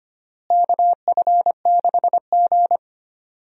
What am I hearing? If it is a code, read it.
Morse code, heard as a keyed tone, about 25 wpm, 700 Hz. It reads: KF6G
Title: USA Call Signs - 4 Characters - Repeated 3x - 25wpm